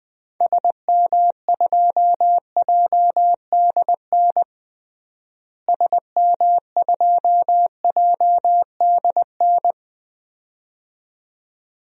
Morse code, heard as a keyed tone, about 21 wpm, 700 Hz. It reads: SM2JDN SM2JDN
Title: DXCC-100 Call Signs - 6 Characters - Repeated 2x - 20wpm